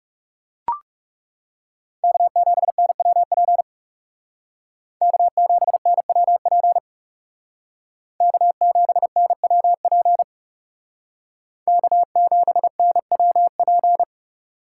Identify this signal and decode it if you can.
Morse code, heard as a keyed tone, about 37 wpm, 700 Hz. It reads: K7NWP K7NWP K7NWP K7NWP
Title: Speed Racing - USA 1x3 Call Signs 30wpm